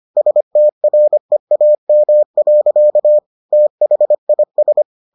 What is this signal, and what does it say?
Morse code, heard as a keyed tone, about 25 wpm, 600 Hz. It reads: STREAM. THIS